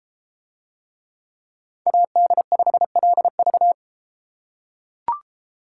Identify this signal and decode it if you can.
Morse code, heard as a keyed tone, about 33 wpm, 700 Hz. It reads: AD5LV